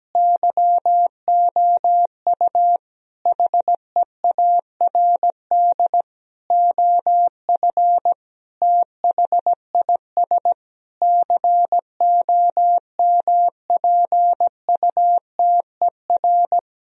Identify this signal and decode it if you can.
Morse code, heard as a keyed tone, about 17 wpm, 700 Hz. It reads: YOU HEARD OF THIS COMPUTER
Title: Sentences from Top 1000 Words - Review Part 3 - 17wpm